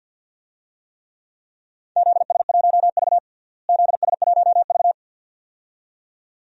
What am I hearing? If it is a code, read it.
Morse code, heard as a keyed tone, about 52 wpm, 700 Hz. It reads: ZS1V ZS1V
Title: DXCC-100 Call Signs - 4 Characters - Repeated 2x - 50wpm